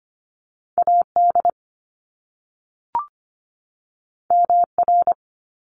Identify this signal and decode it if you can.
Morse code, heard as a keyed tone, about 25 wpm, 700 Hz. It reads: AD MR
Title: Top Two Letter Combinations in Words 25wpm